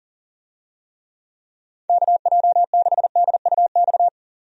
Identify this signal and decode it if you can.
Morse code, heard as a keyed tone, about 40 wpm, 700 Hz. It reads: KJ6DUX
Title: USA Call Signs - 6 Characters - 40wpm